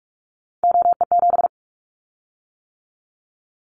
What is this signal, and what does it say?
Morse code, heard as a keyed tone, about 46 wpm, 700 Hz. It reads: OE7
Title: Callsign Prefixes for Top 100 DXCC - 45wpm